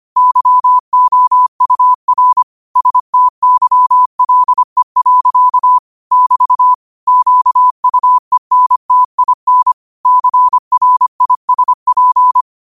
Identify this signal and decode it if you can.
Morse code, heard as a keyed tone, about 25 wpm, 1000 Hz. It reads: YOUR STYLE. <BT> QUENTIN CRISP
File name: qotdfiles / 25wpm